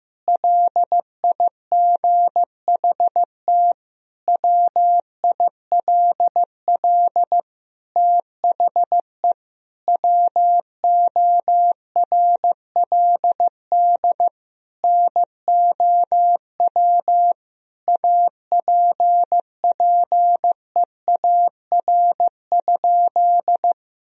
Morse code, heard as a keyed tone, about 16 wpm, 700 Hz. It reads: LIGHT WILL THE WORLD NOW APPEAR?
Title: Sentences from Top 1000 Words - Review Part 2 - 15wpm